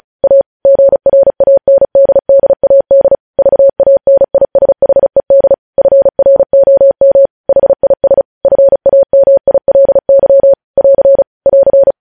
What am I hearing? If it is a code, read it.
Morse code, heard as a keyed tone, about 36 wpm, 550 Hz. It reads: A GRANDDAD VANISHED FROM HIS FAMILY <AR> <AR>
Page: News headlines in Morse code at 35 wpm